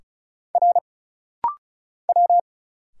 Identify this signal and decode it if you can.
Morse code, heard as a keyed tone, about 37 wpm, 700 Hz. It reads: R W
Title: Single Letters - Rapid-Fire 35wpm